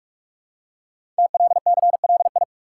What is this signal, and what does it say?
Morse code, heard as a keyed tone, about 44 wpm, 700 Hz. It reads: TLCLI
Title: Code Groups - 5 Letter:Numbers - 45wpm